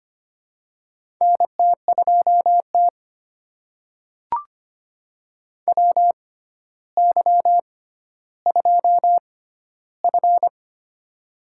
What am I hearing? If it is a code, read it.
Morse code, heard as a keyed tone, about 25 wpm, spaced as for 13 wpm, 700 Hz. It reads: NT2T WY2F
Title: USA Call Signs - 4 Characters - 6x Character Spacing - 25wpm